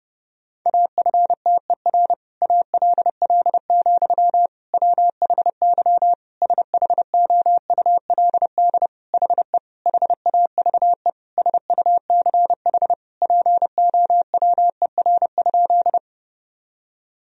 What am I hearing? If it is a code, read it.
Morse code, heard as a keyed tone, about 30 wpm, 700 Hz. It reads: AFTER ALL, WHY SHOULD HE HAVE SUCH POWER?
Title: Sentences from Top 1000 Words - Review Part 2 - 30wpm